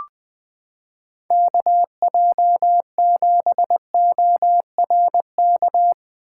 Morse code, heard as a keyed tone, about 20 wpm, 700 Hz. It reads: KJ7ORK
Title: 2023 POTA Activator Call Signs - 20wpm